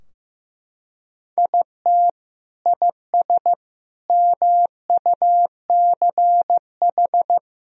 Morse code, heard as a keyed tone, about 15 wpm, 700 Hz. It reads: IT IS MUCH
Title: Sentences from Top 1000 Words - Review Part 3 - 15wpm